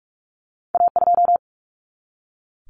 Morse code, heard as a keyed tone, about 46 wpm, 700 Hz. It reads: A2
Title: Common Cut Numbers - 2-Digit 45wpm